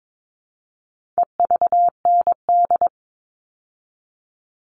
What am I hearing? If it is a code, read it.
Morse code, heard as a keyed tone, about 22 wpm, 700 Hz. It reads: EVND
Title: All Cut Numbers - 4-Digit 22wpm